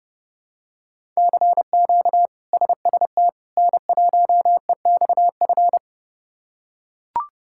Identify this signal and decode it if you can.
Morse code, heard as a keyed tone, about 31 wpm, 700 Hz. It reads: CQ SST N1EXF
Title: K1USN SST QSO Practice - 30wpm